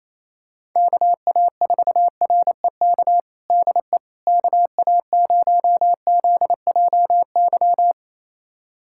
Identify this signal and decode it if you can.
Morse code, heard as a keyed tone, about 28 wpm, 700 Hz. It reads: KA4REK DE KA0ZJY
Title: USA Callsign DE Callsign 28wpm